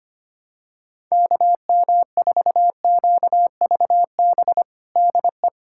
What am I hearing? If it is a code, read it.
Morse code, heard as a keyed tone, about 25 wpm, 700 Hz. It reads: KM4QVB DE
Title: USA Callsign DE Callsign - 6 Characters - 25wpm